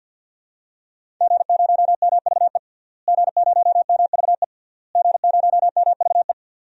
Morse code, heard as a keyed tone, about 49 wpm, 700 Hz. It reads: G0MVE G0MVE G0MVE
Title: DXCC-100 Call Signs - 5 Characters - Repeated 3x - 50wpm